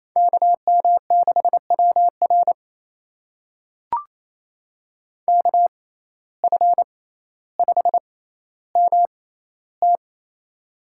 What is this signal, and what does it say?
Morse code, heard as a keyed tone, about 28 wpm, spaced as for 14 wpm, 700 Hz. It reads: KM6WR KF5MT
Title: USA Call Signs - 5 Characters - 6x Character Spacing - 28wpm